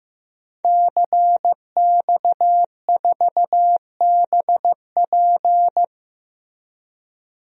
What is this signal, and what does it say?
Morse code, heard as a keyed tone, about 15 wpm, 700 Hz. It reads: CX4BP
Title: DXCC-100 Call Signs - 5 Characters - Repeated 3x - 15wpm